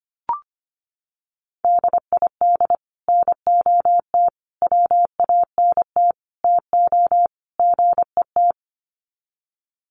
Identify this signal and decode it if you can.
Morse code, heard as a keyed tone, about 25 wpm, 700 Hz. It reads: DID NOT WANT TO GET
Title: Sets of 5 Words Encore2 25wpm